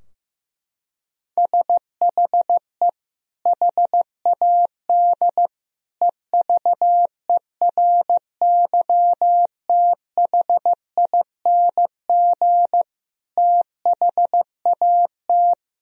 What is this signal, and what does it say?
Morse code, heard as a keyed tone, about 16 wpm, 700 Hz. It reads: SHE HAD EVERYTHING THAT
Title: Sentences from Top 1000 Words - Review Part 4 - 15wpm